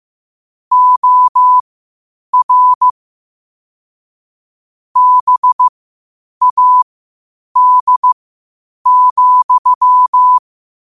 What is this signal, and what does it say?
Morse code, heard as a keyed tone, about 15 wpm, spaced as for 8 wpm, 1000 Hz. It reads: OR BAD,
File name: qotdfiles / 8wpm